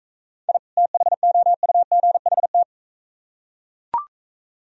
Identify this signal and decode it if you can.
Morse code, heard as a keyed tone, about 41 wpm, 700 Hz. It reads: I THOUGHT